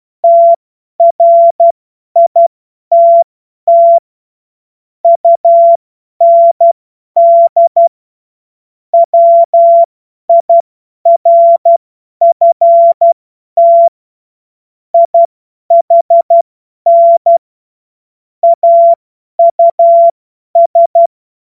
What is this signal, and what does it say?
Morse code, heard as a keyed tone, about 12 wpm, spaced as for 10 wpm, 680 Hz. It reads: TRITT UND WIRFT IHN AUS
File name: Morsekurs / Witze / Witz 07. Juni 2022 50 F70